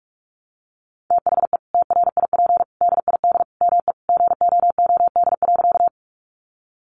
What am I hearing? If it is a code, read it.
Morse code, heard as a keyed tone, about 44 wpm, 700 Hz. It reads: THE TRIP DID ME GOOD.
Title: Sentences from Top 1300 Words 45wpm